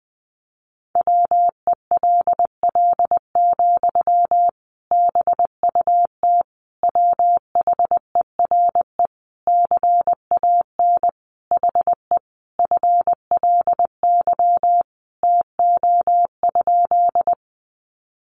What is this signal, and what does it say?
Morse code, heard as a keyed tone, about 20 wpm, 700 Hz. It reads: WELL, BUT WHERE CAN HE FLY TO?
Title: Sentences from Top 500 Words 20wpm